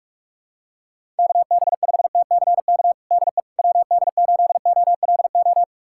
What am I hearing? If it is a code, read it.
Morse code, heard as a keyed tone, about 48 wpm, 700 Hz. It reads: KB5TCK DE WD8OLO
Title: USA Callsign DE Callsign - 6 Characters - 45wpm